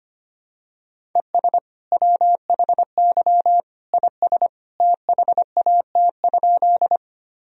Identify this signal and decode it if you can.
Morse code, heard as a keyed tone, about 25 wpm, 700 Hz. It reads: ES WHY IS THAT?
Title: Sentences from Top 200 Words and Abbreviations 25wpm